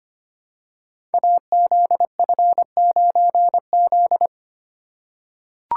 Morse code, heard as a keyed tone, about 25 wpm, 700 Hz. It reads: AZF9Z
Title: Code Groups - 5 Letter:Numbers - 25wpm